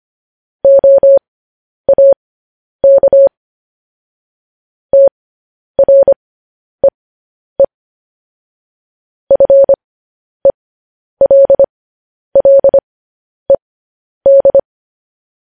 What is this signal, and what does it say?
Morse code, heard as a keyed tone, about 25 wpm, spaced as for 10 wpm, 550 Hz. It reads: OAK TREE FELLED
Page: News Headlines in Morse Code at 10 WPM